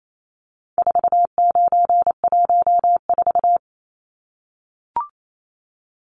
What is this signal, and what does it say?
Morse code, heard as a keyed tone, about 28 wpm, 700 Hz. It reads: 4914
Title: Numbers - 4-Digit 28wpm